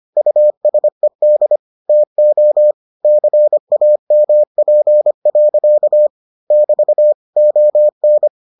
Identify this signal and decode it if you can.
Morse code, heard as a keyed tone, about 25 wpm, 600 Hz. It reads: USED TO CAMP. <BT> ON